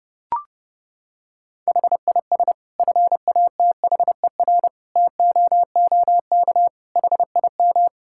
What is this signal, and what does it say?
Morse code, heard as a keyed tone, about 30 wpm, 700 Hz. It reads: HIS FATHER TOOK HIM
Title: Sentences from Top 1000 Words - Review Part 5 - 30wpm